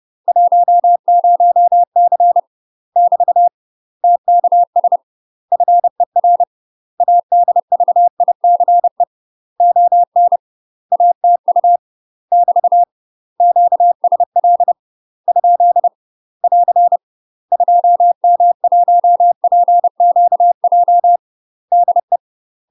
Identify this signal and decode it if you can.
Morse code, heard as a keyed tone, about 30 wpm, 700 Hz. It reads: 10C <BT> TKS FER ADVICE ON ATU <BT> QSL ? <AR> 2M1PQJ DE
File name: UK MASTER QSOs TRAINING/30WPM